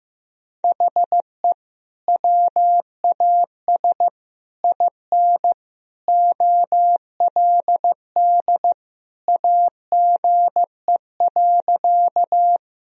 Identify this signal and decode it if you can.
Morse code, heard as a keyed tone, about 16 wpm, 700 Hz. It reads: HE WAS IN OLD AGE.
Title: Sentences from Top 1000 Words - Review Part 3 - 15wpm